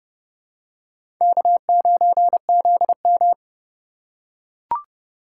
Morse code, heard as a keyed tone, about 32 wpm, 700 Hz. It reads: K9ZM
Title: USA Call Signs - 4 Characters - 3x Character Spacing - 30wpm